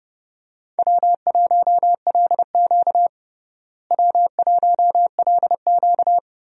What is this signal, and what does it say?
Morse code, heard as a keyed tone, about 30 wpm, 700 Hz. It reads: W1LQ W1LQ
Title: USA Call Signs - 4 Characters - Repeated 3x - 30wpm